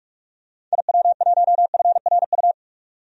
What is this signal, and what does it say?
Morse code, heard as a keyed tone, about 46 wpm, 700 Hz. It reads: IW1FRU
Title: DXCC-100 Call Signs - 6 Characters - 45wpm